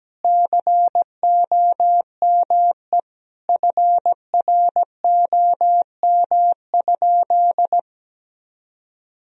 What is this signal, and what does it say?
Morse code, heard as a keyed tone, about 17 wpm, 700 Hz. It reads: COME FROM?
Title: Sentences from Top 1000 Words - Review Part 1 - 17wpm